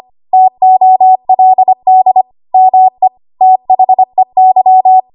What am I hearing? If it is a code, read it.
Morse code, heard as a keyed tone, about 25 wpm, 750 Hz. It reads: TOLD ME THEY
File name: Morse / 25WPM